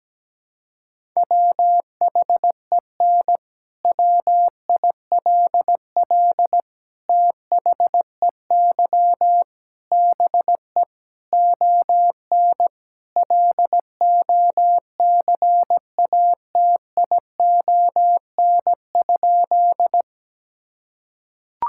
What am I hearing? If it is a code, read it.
Morse code, heard as a keyed tone, about 17 wpm, 700 Hz. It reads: WHEN WILL THEY BE ON LOCATION?
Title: Sentences from Top 1000 Words - Review Part 6 - 17wpm